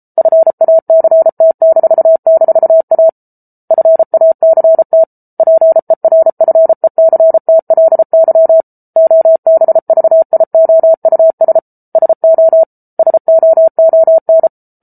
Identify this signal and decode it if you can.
Morse code, heard as a keyed tone, about 34 wpm, 650 Hz. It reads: FACT--A FACT PERFECTLY OBVIOUS SO SOON